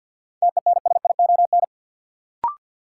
Morse code, heard as a keyed tone, about 48 wpm, 700 Hz. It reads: TENSION
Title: Sentences from Top 1600 Words - New Word Drill - 50wpm